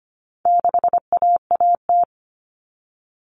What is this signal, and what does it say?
Morse code, heard as a keyed tone, about 25 wpm, 700 Hz. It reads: 6AAT
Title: Common Cut Numbers - 4-Digit 25wpm